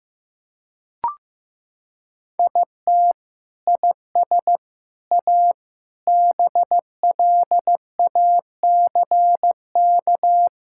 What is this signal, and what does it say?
Morse code, heard as a keyed tone, about 15 wpm, 700 Hz. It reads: IT IS A BLACK
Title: Sentences from Top 1600 Words 15wpm